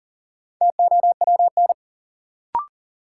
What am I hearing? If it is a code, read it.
Morse code, heard as a keyed tone, about 40 wpm, 700 Hz. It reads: TOWN
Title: Sentences from Top 500 Words - New Word Drill - 40wpm